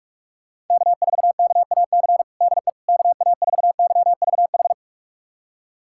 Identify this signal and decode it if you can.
Morse code, heard as a keyed tone, about 42 wpm, 690 Hz. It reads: K4KAC DE KA4YVH
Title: USA Callsign DE Callsign 45wpm